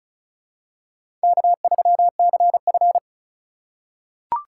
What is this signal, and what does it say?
Morse code, heard as a keyed tone, about 36 wpm, 700 Hz. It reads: K3CF
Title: USA Call Signs - 4 Characters - 3x Character Spacing - 35wpm